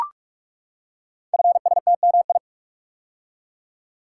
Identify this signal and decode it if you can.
Morse code, heard as a keyed tone, about 44 wpm, 690 Hz. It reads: FSTMI
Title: Code Groups - 5 Letters - 45wpm